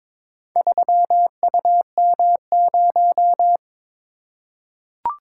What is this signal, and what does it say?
Morse code, heard as a keyed tone, about 22 wpm, 700 Hz. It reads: 3UM0
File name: practice / 22wpm / Four Letter-Number Combinations - 22wpm